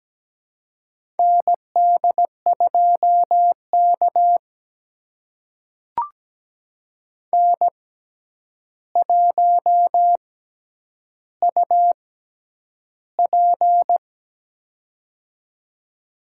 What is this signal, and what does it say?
Morse code, heard as a keyed tone, about 17 wpm, spaced as for 9 wpm, 700 Hz. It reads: ND2K N1UP
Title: USA Call Signs - 4 Characters - 6x Character Spacing - 17wpm